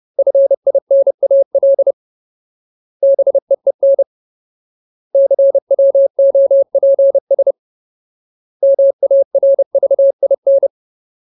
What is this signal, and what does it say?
Morse code, heard as a keyed tone, about 30 wpm, 550 Hz. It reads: FINAL BEEN CWOPS MARVIN
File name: Ultimate Word List / 30WPM Ultimate